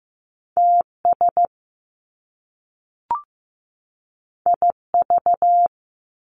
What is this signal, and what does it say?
Morse code, heard as a keyed tone, about 15 wpm, 700 Hz. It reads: TS IV
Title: Top Two Letter Combinations in Words 15wpm